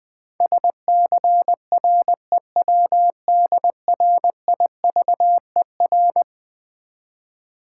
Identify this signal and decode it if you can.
Morse code, heard as a keyed tone, about 20 wpm, 700 Hz. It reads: SCREWDRIVER
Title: Easy Words with Hints - 20wpm